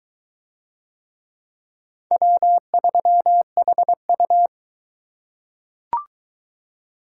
Morse code, heard as a keyed tone, about 22 wpm, 700 Hz. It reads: W3HU